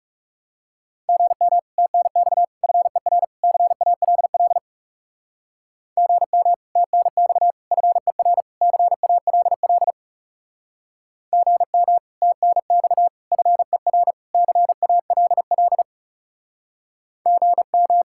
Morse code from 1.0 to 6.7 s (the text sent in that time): GM TNX FER CALL GM